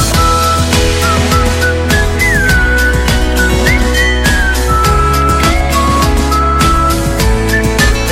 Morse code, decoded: none